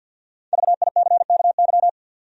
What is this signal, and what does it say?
Morse code, heard as a keyed tone, about 53 wpm, 700 Hz. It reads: VICKY
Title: USA Common Names - 50wpm